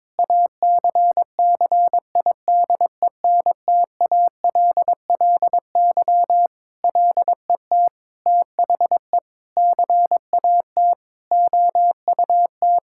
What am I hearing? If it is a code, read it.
Morse code, heard as a keyed tone, about 23 wpm, 700 Hz. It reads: ACCIDENTALLY LET THE CAT OUT
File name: Sentences with 7-Word Phrases - 22wpm